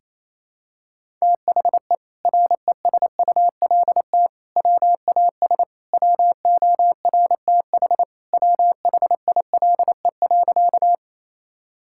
Morse code, heard as a keyed tone, about 28 wpm, 700 Hz. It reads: THE RESULT WAS WORTH WHILE.
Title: Sentences from Top 1000 Words - Review Part 5 - 28wpm